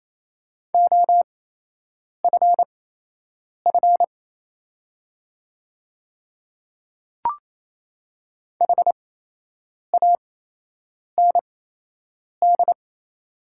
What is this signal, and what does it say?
Morse code, heard as a keyed tone, about 27 wpm, spaced as for 8 wpm, 700 Hz. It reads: OFF HAND
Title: 8x Character Spacing - Top 300 Words - 28wpm